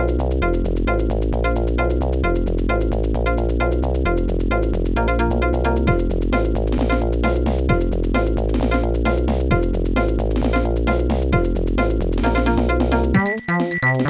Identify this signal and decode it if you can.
There is no code.